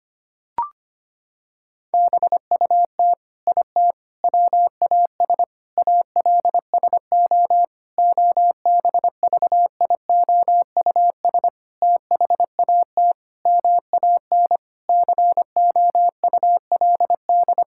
Morse code, heard as a keyed tone, about 25 wpm, 700 Hz. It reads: BUT IT WAS ALSO OBVIOUS THAT MAN COULD
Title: Sentences from Top 1600 Words 25wpm